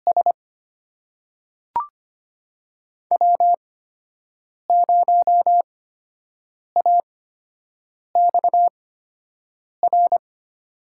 Morse code, heard as a keyed tone, about 25 wpm, spaced as for 7 wpm, 700 Hz. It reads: S W0AXR
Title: USA Call Signs - 5 Characters - 8x Character Spacing - 25wpm